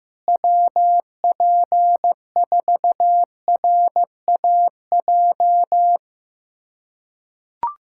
Morse code, heard as a keyed tone, about 15 wpm, 700 Hz. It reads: WP4RAJ
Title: USA Callsign DE Callsign - 6 Characters - 15wpm